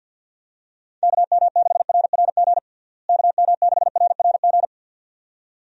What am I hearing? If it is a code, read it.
Morse code, heard as a keyed tone, about 51 wpm, 700 Hz. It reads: KM6RRG KM6RRG
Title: USA Call Signs - 6 Characters - Repeated 2x - 50wpm